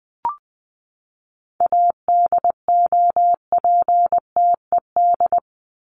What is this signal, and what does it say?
Morse code, heard as a keyed tone, about 20 wpm, 700 Hz. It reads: ADOPTED
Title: ED Suffix - Top 100 Words with Base Words 20wpm